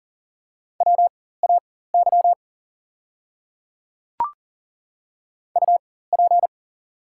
Morse code, heard as a keyed tone, about 40 wpm, spaced as for 19 wpm, 690 Hz. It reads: WAY UP
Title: 4x Character Spacing - Top 100 Words - 40wpm